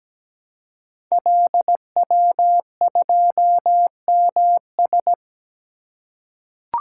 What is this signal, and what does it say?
Morse code, heard as a keyed tone, about 17 wpm, 700 Hz. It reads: LW2MS
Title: Code Groups - 5 Letter:Numbers - 17wpm